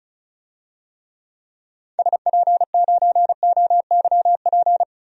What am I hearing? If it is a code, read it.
Morse code, heard as a keyed tone, about 36 wpm, 700 Hz. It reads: SP9OYP
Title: DXCC-100 Call Signs - 6 Characters - 35wpm